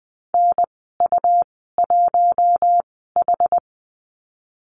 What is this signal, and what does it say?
Morse code, heard as a keyed tone, about 20 wpm, spaced as for 15 wpm, 700 Hz. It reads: NU1H
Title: 122 Lesson 20g - Number 3 - Call Signs 20wpm